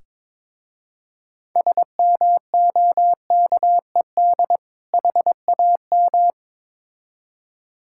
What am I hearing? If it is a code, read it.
Morse code, heard as a keyed tone, about 22 wpm, 700 Hz. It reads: SMOKED HAM
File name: Difficult Words with Hints - 22wpm